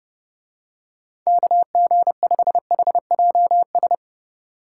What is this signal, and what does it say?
Morse code, heard as a keyed tone, about 28 wpm, 700 Hz. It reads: KG5HJS
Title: Speed Racing - USA 2x3 Call Signs 30wpm